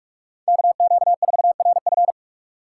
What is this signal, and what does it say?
Morse code, heard as a keyed tone, about 44 wpm, 690 Hz. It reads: KQ4RF
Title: USA Call Signs - 5 Characters - 6x Character Spacing - 45wpm